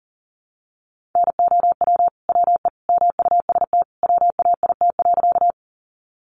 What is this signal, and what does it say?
Morse code, heard as a keyed tone, about 40 wpm, 700 Hz. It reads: NOW WE MUST WAIT.
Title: Sentences from Top 500 Words 40wpm